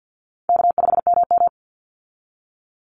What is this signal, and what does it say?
Morse code, heard as a keyed tone, about 50 wpm, 700 Hz. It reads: K5MM
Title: Easy Call Signs Repeated 2x - 50wpm